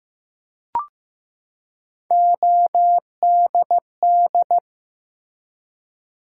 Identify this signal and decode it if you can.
Morse code, heard as a keyed tone, about 15 wpm, 700 Hz. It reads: ODD
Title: Three Letter Words - 15wpm